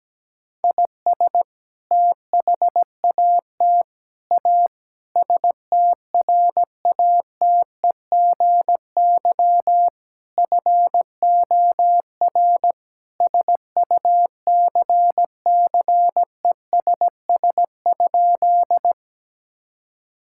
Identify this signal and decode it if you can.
Morse code, heard as a keyed tone, about 17 wpm, 700 Hz. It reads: IS THAT A STRATEGY FOR SUCCESS?
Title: Sentences from Top 1000 Words - Review Part 5 - 17wpm